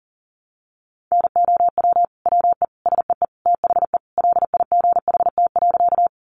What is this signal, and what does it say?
Morse code, heard as a keyed tone, about 40 wpm, 700 Hz. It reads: NOW WE SEE THE LIGHT.
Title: Sentences from Top 200 Words 40wpm